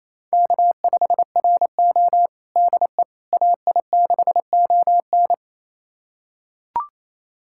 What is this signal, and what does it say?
Morse code, heard as a keyed tone, about 28 wpm, 700 Hz. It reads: K5RO DE AI6ON
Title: USA Callsign DE Callsign 28wpm